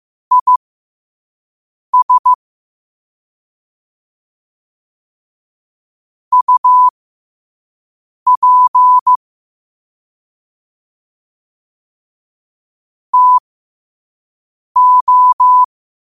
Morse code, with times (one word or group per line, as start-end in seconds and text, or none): 0.3-2.3 IS
6.3-9.2 UP
13.1-15.6 TO